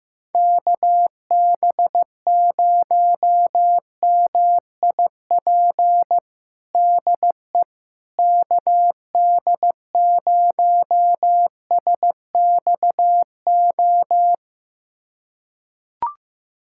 Morse code, heard as a keyed tone, about 15 wpm, 700 Hz. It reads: KB0MIP DE KD0SXO
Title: USA Callsign DE Callsign 15wpm